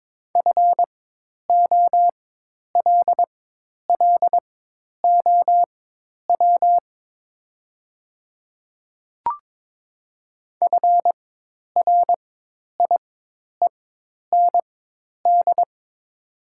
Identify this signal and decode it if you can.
Morse code, heard as a keyed tone, about 22 wpm, spaced as for 10 wpm, 700 Hz. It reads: FOLLOW FRIEND
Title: 4x Character Spacing - Top 300 Words - 22wpm